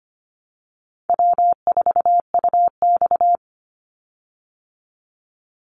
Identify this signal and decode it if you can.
Morse code, heard as a keyed tone, about 25 wpm, 700 Hz. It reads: W4UX
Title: Easy Call Signs Repeated 3x - 25wpm